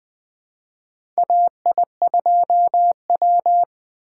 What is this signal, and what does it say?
Morse code, heard as a keyed tone, about 21 wpm, 700 Hz. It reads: AI2W
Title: USA Call Signs - 4 Characters - 4x Character Spacing - 20wpm